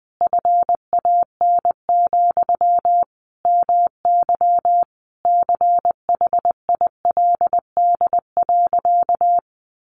Morse code, heard as a keyed tone, about 20 wpm, 700 Hz. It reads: FAN, MY CHILD.
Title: Sentences from Top 1300 Words 20wpm